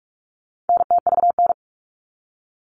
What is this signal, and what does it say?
Morse code, heard as a keyed tone, about 44 wpm, 700 Hz. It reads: NTVN
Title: All Cut Numbers - 4-Digit 45wpm